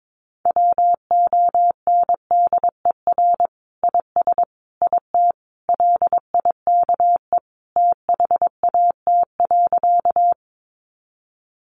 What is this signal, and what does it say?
Morse code, heard as a keyed tone, about 22 wpm, 700 Hz. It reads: WONDER IS IT LIKE THAT.
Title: Sentences from Top 500 Words 22wpm